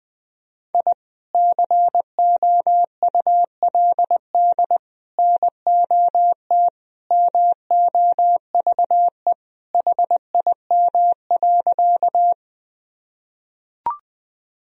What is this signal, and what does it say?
Morse code, heard as a keyed tone, about 20 wpm, 700 Hz. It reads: I COULD NOT MOVE HIM.
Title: Sentences from Top 1000 Words - Review Part 1 - 20wpm